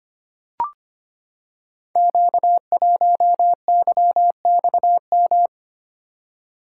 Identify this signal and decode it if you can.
Morse code, heard as a keyed tone, about 25 wpm, 700 Hz. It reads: Q1YXM
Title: Code Groups - 5 Letter:Numbers - 25wpm